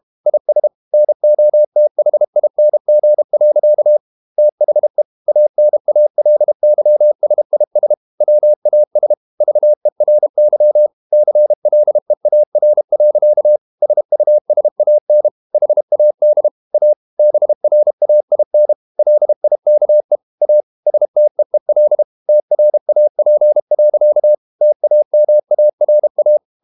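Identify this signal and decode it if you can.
Morse code, heard as a keyed tone, about 32 wpm, 600 Hz. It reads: IS NOTHING. THE ANALYSIS WAS VERY CLEAR. SUSAN HAD A BRAIN LIKE A STEEL TRAP. TAMARA
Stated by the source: Morse code practice at 32 wpm from CW med Gnister, reading Ken Follett's novel Never in English.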